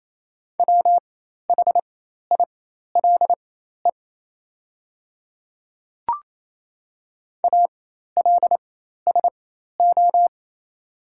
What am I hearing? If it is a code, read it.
Morse code, heard as a keyed tone, about 27 wpm, spaced as for 13 wpm, 700 Hz. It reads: WHILE ALSO
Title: 4x Character Spacing - Top 300 Words - 28wpm